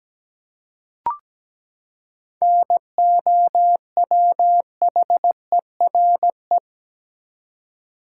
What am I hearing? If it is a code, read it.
Morse code, heard as a keyed tone, about 17 wpm, 700 Hz. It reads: NOWHERE
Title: Seven Letter Words - 17wpm